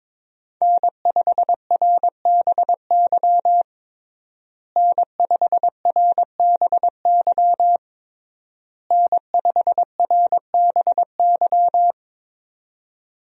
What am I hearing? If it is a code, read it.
Morse code, heard as a keyed tone, about 22 wpm, 700 Hz. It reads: N5RBY N5RBY N5RBY
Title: USA Call Signs - 5 Characters - Repeated 3x - 22wpm